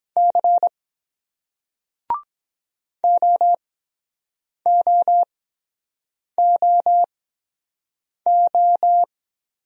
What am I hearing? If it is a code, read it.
Morse code, heard as a keyed tone, about 23 wpm, spaced as for 7 wpm, 700 Hz. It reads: C OOOO